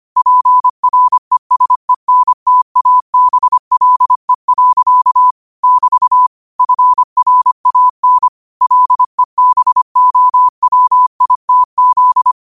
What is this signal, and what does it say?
Morse code, heard as a keyed tone, about 25 wpm, 1000 Hz. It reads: PRESENTABLE. <BT> FRAN LEBOWITZ
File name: qotdfiles / 25wpm